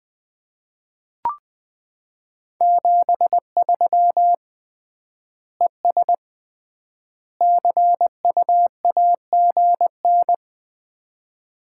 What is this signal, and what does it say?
Morse code, heard as a keyed tone, about 20 wpm, 700 Hz. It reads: 73 ES CU AGN
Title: Common QSO Phrases - 3x Word-Spacing - 20wpm